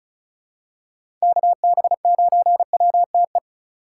Morse code, heard as a keyed tone, about 36 wpm, 700 Hz. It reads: KB9WTE
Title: USA Call Signs - 6 Characters - 2x Character Spacing - 35wpm